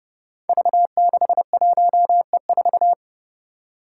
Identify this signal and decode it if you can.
Morse code, heard as a keyed tone, about 30 wpm, 700 Hz. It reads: V61E4
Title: Five Challenging Letter-Number Combinations 30wpm